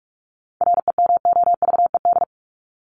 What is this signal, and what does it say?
Morse code, heard as a keyed tone, about 44 wpm, 700 Hz. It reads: REMOVED
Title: ED Suffix - Top 100 Words with Base Words 45wpm